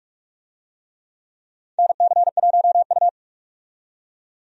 Morse code, heard as a keyed tone, about 43 wpm, 700 Hz. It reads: NC1U
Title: USA Call Signs - 4 Characters - 45wpm